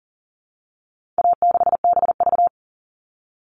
Morse code, heard as a keyed tone, about 40 wpm, 700 Hz. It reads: A6BV
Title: All Cut Numbers - 4-Digit 40wpm